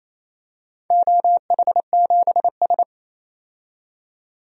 Morse code, heard as a keyed tone, about 28 wpm, 700 Hz. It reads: OH7S
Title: DXCC-100 Call Signs - 4 Characters - 28wpm